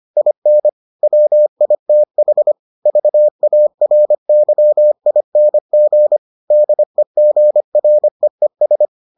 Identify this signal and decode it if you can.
Morse code, heard as a keyed tone, about 25 wpm, 600 Hz. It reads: IN WITH VARYING DEGREES